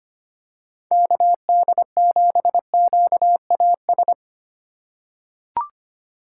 Morse code, heard as a keyed tone, about 25 wpm, 700 Hz. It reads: KD7QAS
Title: USA Call Signs - 6 Characters - 4x Character Spacing - 25wpm